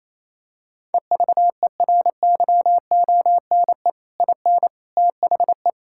Morse code, heard as a keyed tone, about 28 wpm, 700 Hz. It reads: EVERYONE IN THE
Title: Sentences from Top 1000 Words - Review Part 6 - 28wpm